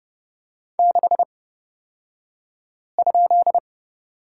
Morse code, heard as a keyed tone, about 30 wpm, spaced as for 5 wpm, 700 Hz. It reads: 6?
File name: Lesson 38d - Letter q - Two Characters 30wpm